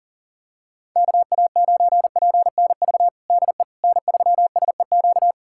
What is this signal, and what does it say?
Morse code, heard as a keyed tone, about 40 wpm, 700 Hz. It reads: KA9PNV DE N3SEQ
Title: USA Callsign DE Callsign 40wpm